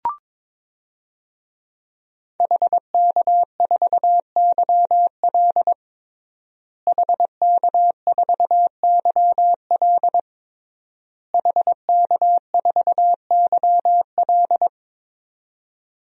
Morse code, heard as a keyed tone, about 21 wpm, 700 Hz. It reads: HK4YL HK4YL HK4YL
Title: DXCC-100 Call Signs - 5 Characters - Repeated 3x - 22wpm